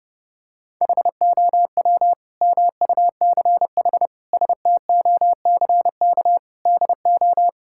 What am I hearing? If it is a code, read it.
Morse code, heard as a keyed tone, about 30 wpm, 700 Hz. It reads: HOW MUCH STOCK DO
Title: Sentences from Top 1000 Words - Review Part 5 - 30wpm